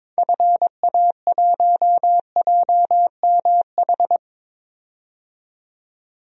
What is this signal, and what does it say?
Morse code, heard as a keyed tone, about 22 wpm, 700 Hz. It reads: FA1JMH
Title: DXCC-100 Call Signs - 6 Characters - 22wpm